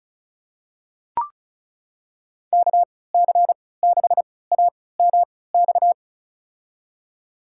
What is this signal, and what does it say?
Morse code, heard as a keyed tone, about 36 wpm, spaced as for 20 wpm, 700 Hz. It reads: KC6AMX
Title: USA Call Signs - 6 Characters - 3x Character Spacing - 35wpm